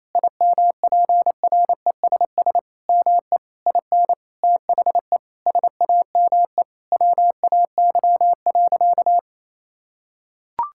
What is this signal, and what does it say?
Morse code, heard as a keyed tone, about 28 wpm, 700 Hz. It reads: IMPRESS ME IN THE SAME WAY.
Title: Sentences from Top 1600 Words 28wpm